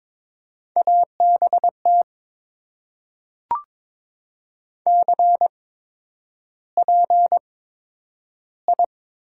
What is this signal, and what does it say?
Morse code, heard as a keyed tone, about 22 wpm, spaced as for 9 wpm, 700 Hz. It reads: ABT CPI
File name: Top 90 QSO Elements - 8x Character Spacing - 22wpm